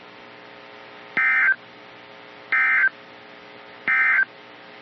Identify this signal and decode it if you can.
Emergency Alert System (SAME) tones, heard as harsh buzzing bursts. It is heard from 1.0 to 4.3 s.
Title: NOAA Weather Radio - KEC55